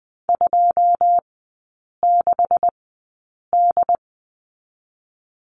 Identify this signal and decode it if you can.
Morse code, heard as a keyed tone, about 20 wpm, spaced as for 8 wpm, 700 Hz. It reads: 26D
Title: 198 Lesson 32e - Letter k - Three Characters 20wpm